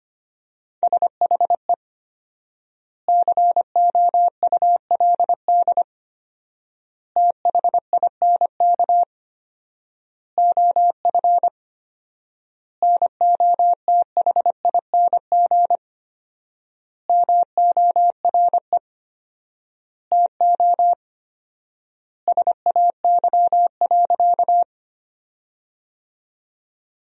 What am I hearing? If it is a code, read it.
Morse code, heard as a keyed tone, about 25 wpm, 700 Hz. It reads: SHE COULD THINK OF NOTHING MORE TO SAY.
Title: 4x Word-Spacing - Sentences from Top 500 Words - 25wpm